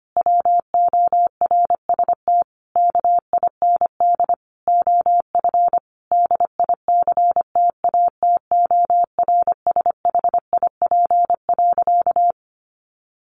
Morse code, heard as a keyed tone, about 25 wpm, 700 Hz. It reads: WORST KIND OF DICTATORSHIP.
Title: A Sound Of Thunder 25wpm